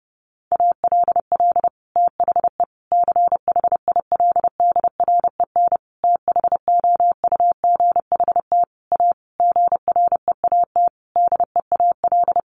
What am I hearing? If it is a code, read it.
Morse code, heard as a keyed tone, about 30 wpm, 700 Hz. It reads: ALL THE CHILDREN THOUGHT A GREAT DEAL
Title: Sentences from Top 800 Words 30wpm